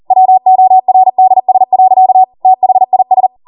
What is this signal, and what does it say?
Morse code, heard as a keyed tone, about 40 wpm, 750 Hz. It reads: WORDS. THIS